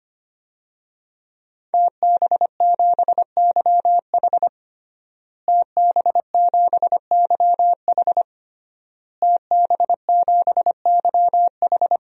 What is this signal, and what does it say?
Morse code, heard as a keyed tone, about 25 wpm, 700 Hz. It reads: TB7YH TB7YH TB7YH
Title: DXCC-100 Call Signs - 5 Characters - Repeated 3x - 25wpm